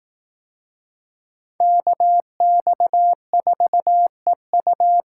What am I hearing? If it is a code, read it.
Morse code, heard as a keyed tone, about 18 wpm, 700 Hz. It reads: KX4EU
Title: Speed Racing - USA 2x2 Call Signs 15wpm